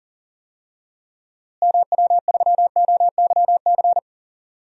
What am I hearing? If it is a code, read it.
Morse code, heard as a keyed tone, about 40 wpm, 700 Hz. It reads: MW3OYC
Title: DXCC-100 Call Signs - 6 Characters - Repeated 2x - 40wpm